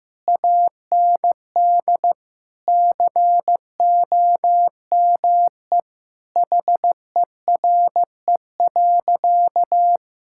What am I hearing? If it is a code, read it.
Morse code, heard as a keyed tone, about 15 wpm, 700 Hz. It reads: AND COME HERE.
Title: Sentences from Top 1600 Words 15wpm